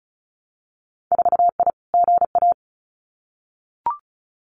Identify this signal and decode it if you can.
Morse code, heard as a keyed tone, about 36 wpm, 700 Hz. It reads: 4I GA
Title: Winter Field Day Exchanges Encore 35wpm